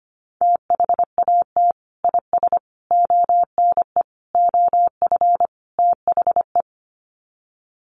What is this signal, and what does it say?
Morse code, heard as a keyed tone, about 25 wpm, 700 Hz. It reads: THAT IS ONE OF THE
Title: Sets of 5 Words Encore 25wpm